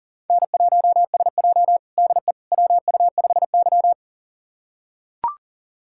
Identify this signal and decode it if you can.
Morse code, heard as a keyed tone, about 40 wpm, 700 Hz. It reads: N1SJ DE WU5Y
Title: USA Callsign DE Callsign - 4 Characters - 40wpm